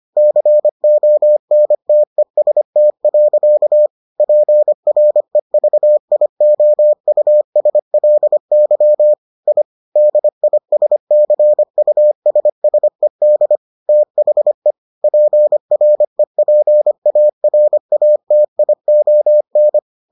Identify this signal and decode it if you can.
Morse code, heard as a keyed tone, about 25 wpm, 600 Hz. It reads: CONTEST. PREVIOUSLY I DISCUSSED THE PREPARATION